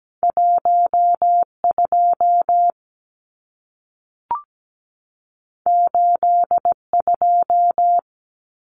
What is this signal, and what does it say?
Morse code, heard as a keyed tone, about 17 wpm, 700 Hz. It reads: 12 82
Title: Numbers - 2-Digit 17wpm